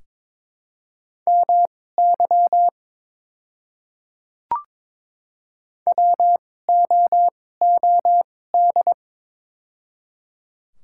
Morse code, heard as a keyed tone, about 22 wpm, spaced as for 16 wpm, 700 Hz. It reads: MY WOOD
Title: 2x Character Spacing - Top 300 Words - 22wpm